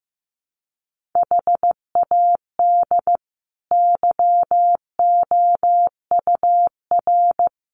Morse code, heard as a keyed tone, about 15 wpm, 700 Hz. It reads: HAD YOUR
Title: Sentences from Top 1500 Words 15wpm